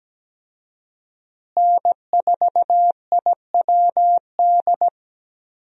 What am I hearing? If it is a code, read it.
Morse code, heard as a keyed tone, about 17 wpm, 700 Hz. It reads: N4IWD
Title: USA Call Signs - 5 Characters - Repeated 3x - 17wpm